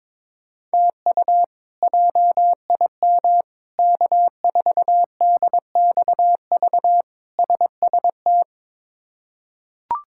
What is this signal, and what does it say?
Morse code, heard as a keyed tone, about 22 wpm, 700 Hz. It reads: TU JIM K4DXV SST
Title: K1USN SST QSO Practice - 22wpm